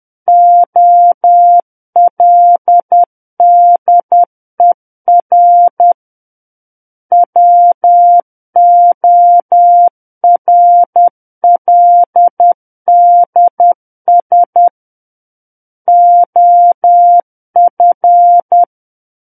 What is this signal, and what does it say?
Morse code, heard as a keyed tone, about 10 wpm, 700 Hz. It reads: OLDER WORLDS OF
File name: Book One / 01-Chapter 1 - 10 WPM